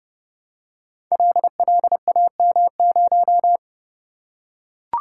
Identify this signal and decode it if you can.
Morse code, heard as a keyed tone, about 30 wpm, 700 Hz. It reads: LLAM0
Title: Code Groups - 5 Letter:Numbers - 30wpm